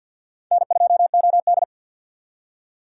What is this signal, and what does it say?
Morse code, heard as a keyed tone, about 50 wpm, 700 Hz. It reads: N2OD
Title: USA Call Signs - 6x Character Spacing - 50wpm